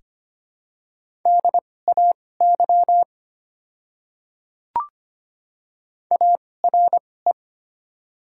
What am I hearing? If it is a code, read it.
Morse code, heard as a keyed tone, about 25 wpm, spaced as for 18 wpm, 700 Hz. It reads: DAY ARE
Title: 2x Character Spacing - Top 300 Words - 25wpm